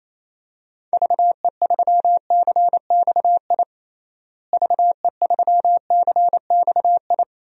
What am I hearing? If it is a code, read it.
Morse code, heard as a keyed tone, about 28 wpm, 700 Hz. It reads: VE3CXI VE3CXI
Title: DXCC-100 Call Signs - 6 Characters - Repeated 3x - 28wpm